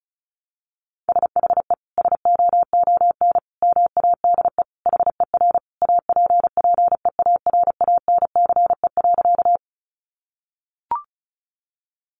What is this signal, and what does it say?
Morse code, heard as a keyed tone, about 34 wpm, 700 Hz. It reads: SHE SOON MADE HER APPEARANCE.
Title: Sentences from Top 1100 Words 35wpm